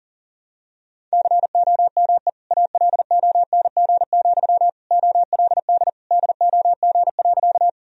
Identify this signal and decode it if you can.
Morse code, heard as a keyed tone, about 40 wpm, 700 Hz. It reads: COME ALONG, OLD DOG.
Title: Sentences from Top 1000 Words - Review Part 2 - 40wpm